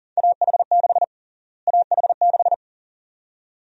Morse code, heard as a keyed tone, about 37 wpm, 700 Hz. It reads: AH6 AH6
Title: USA Call Sign Prefixes - Repeated 2x - 40wpm